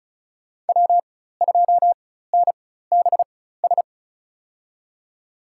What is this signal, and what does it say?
Morse code, heard as a keyed tone, about 36 wpm, spaced as for 16 wpm, 700 Hz. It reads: W2NBS
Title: USA Call Signs - 4x Character Spacing - 35wpm